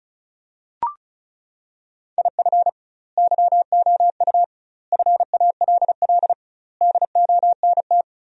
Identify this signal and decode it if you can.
Morse code, heard as a keyed tone, about 36 wpm, 700 Hz. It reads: IF YOU FALL DONT
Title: 2x Word-Spacing - Sentences from Top 500 Words - 35wpm